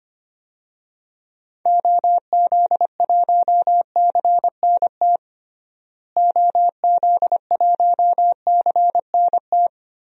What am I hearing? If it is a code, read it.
Morse code, heard as a keyed tone, about 25 wpm, 700 Hz. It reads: OZ1CNT OZ1CNT
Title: DXCC-100 Call Signs - 6 Characters - Repeated 2x - 25wpm